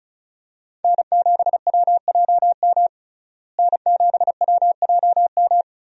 Morse code, heard as a keyed tone, about 36 wpm, 700 Hz. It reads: N7WJM N7WJM
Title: USA Call Signs - 5 Characters - Repeated 2x - 35wpm